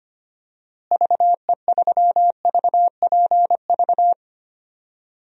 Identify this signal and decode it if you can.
Morse code, heard as a keyed tone, about 25 wpm, 700 Hz. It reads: VE3VPV
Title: DXCC-100 Call Signs - 6 Characters - Repeated 3x - 25wpm